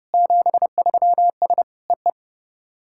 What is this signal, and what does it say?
Morse code, heard as a keyed tone, about 30 wpm, 700 Hz. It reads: 73S EE
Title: SOTA QSO Practice 30wpm